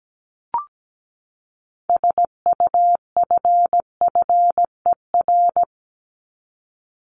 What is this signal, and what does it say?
Morse code, heard as a keyed tone, about 17 wpm, 700 Hz. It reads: SUFFER
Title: Top 1501-1750 Words 17wpm